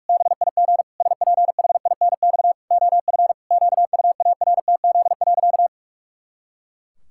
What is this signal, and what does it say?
Morse code, heard as a keyed tone, about 42 wpm, 700 Hz. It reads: BIG SPHINX OF QUARTZ.
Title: Pangram Challenge v2 - 45wpm